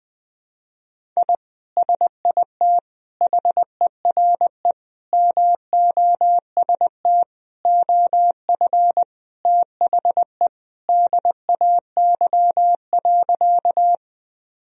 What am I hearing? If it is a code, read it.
Morse code, heard as a keyed tone, about 20 wpm, 700 Hz. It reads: I SIT HERE MOST OF THE DAY.
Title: Sentences from Top 1000 Words - Review Part 1 - 20wpm